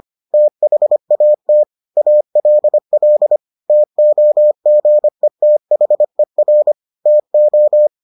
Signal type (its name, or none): Morse code